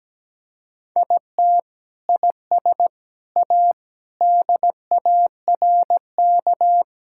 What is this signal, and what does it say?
Morse code, heard as a keyed tone, about 17 wpm, 700 Hz. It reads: IT IS A DARK
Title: Sentences from Top 1000 Words - Review Part 5 - 17wpm